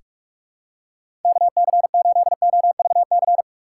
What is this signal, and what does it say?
Morse code, heard as a keyed tone, about 42 wpm, 700 Hz. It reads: KC8OVC
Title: USA Call Signs - 6 Characters - 4x Character Spacing - 45wpm